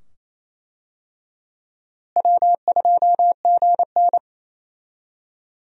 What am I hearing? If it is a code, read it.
Morse code, heard as a keyed tone, about 27 wpm, 700 Hz. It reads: W2GN
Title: USA Call Signs - 4 Characters - Repeated 2x - 28wpm